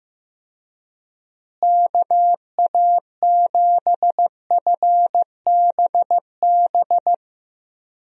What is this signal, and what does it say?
Morse code, heard as a keyed tone, about 15 wpm, 700 Hz. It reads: KA7FBB
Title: USA Call Signs - 6 Characters - Repeated 2x - 15wpm